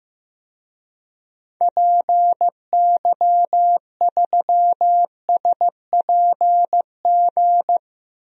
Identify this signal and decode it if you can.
Morse code, heard as a keyed tone, about 15 wpm, 700 Hz. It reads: PY3SPG
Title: DXCC-100 Call Signs - 6 Characters - Repeated 2x - 15wpm